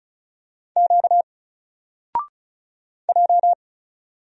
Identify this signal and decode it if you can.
Morse code, heard as a keyed tone, about 36 wpm, 700 Hz. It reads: Q J
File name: Single-Letter-Number-35wpm